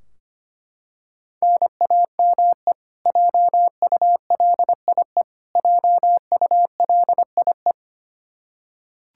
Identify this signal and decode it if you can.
Morse code, heard as a keyed tone, about 25 wpm, 700 Hz. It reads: NAME JULIE JULIE
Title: Practice QSO - Exchange 1 - Beginning - 25wpm